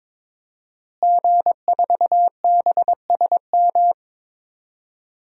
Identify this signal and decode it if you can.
Morse code, heard as a keyed tone, about 22 wpm, 700 Hz. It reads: G4BSM
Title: DXCC-100 Call Signs - 5 Characters - Repeated 3x - 22wpm